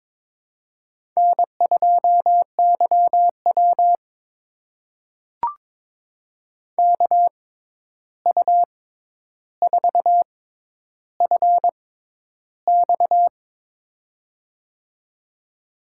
Morse code, heard as a keyed tone, about 22 wpm, spaced as for 11 wpm, 700 Hz. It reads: N2YW KU4FX
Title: USA Call Signs - 6x Character Spacing - 22wpm